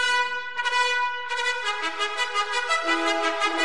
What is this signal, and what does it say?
Emergency Alert System (SAME) tones, heard as harsh buzzing bursts. It is heard from 0.3 to 2.9 s.